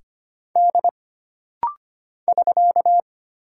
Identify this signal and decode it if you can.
Morse code, heard as a keyed tone, about 25 wpm, 700 Hz. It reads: D <SK>
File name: Single Character (Letter:Number:Punctuation:Prosign) - Rapid-Fire - 25wpm